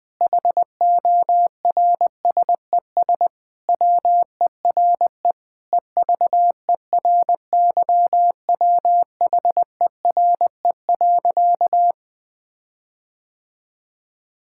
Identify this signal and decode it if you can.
Morse code, heard as a keyed tone, about 20 wpm, 700 Hz. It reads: HORSES WERE EVERYWHERE.
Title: Sentences with 8-Word Phrases - Spoken Beginning - 20wpm